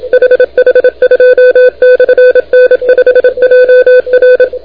Morse code, heard as a keyed tone, about 27 wpm, 500 Hz. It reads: HH2/N5JR
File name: recordings / HH2-N5JR de NT1K ARRL10M CW 28MHz